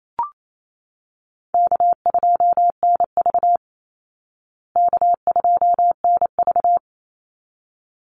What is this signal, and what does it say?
Morse code, heard as a keyed tone, about 28 wpm, 700 Hz. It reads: K2NV K2NV
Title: Easy Call Signs Repeated 2x - 28wpm